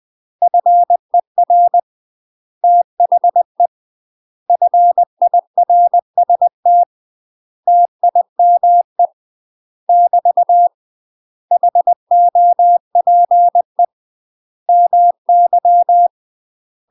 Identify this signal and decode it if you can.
Morse code, heard as a keyed tone, about 20 wpm, 700 Hz. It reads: FER THE FIRST TIME <BT> HOPE MY
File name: UK MASTER QSOs TRAINING / 20WPM / UK MASTER QSOs 0026